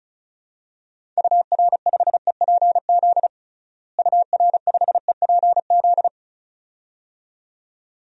Morse code, heard as a keyed tone, about 36 wpm, 700 Hz. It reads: UR5EPZ UR5EPZ
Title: DXCC-100 Call Signs - 6 Characters - Repeated 2x - 35wpm